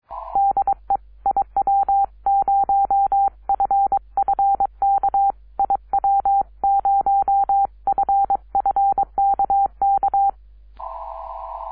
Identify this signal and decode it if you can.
Morse code, heard as a keyed tone, about 23 wpm, 790 Hz. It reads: DE IW0FFK IW0FFK K
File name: iw0ffk / 2006 06 10 n8pr iw0ffk